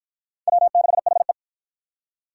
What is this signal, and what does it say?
Morse code, heard as a keyed tone, about 45 wpm, 700 Hz. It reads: W6HE